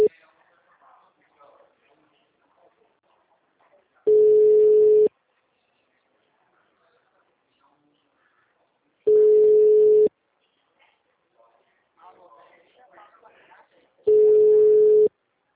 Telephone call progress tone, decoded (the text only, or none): ringback tone